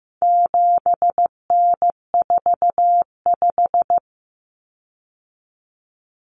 Morse code, heard as a keyed tone, about 14 wpm, 700 Hz. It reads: 7N45
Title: Common Cut Numbers - 4-Digit 15wpm